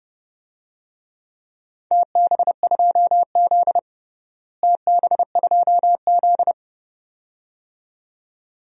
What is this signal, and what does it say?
Morse code, heard as a keyed tone, about 32 wpm, 700 Hz. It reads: TB2Z TB2Z
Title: DXCC-100 Call Signs - 4 Characters - Repeated 2x - 30wpm